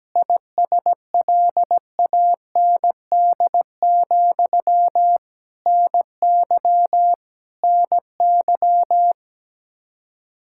Morse code, heard as a keyed tone, about 17 wpm, 700 Hz. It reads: ISLAND, NY NY
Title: Practice QSO - Exchange 1 - Beginning - 17wpm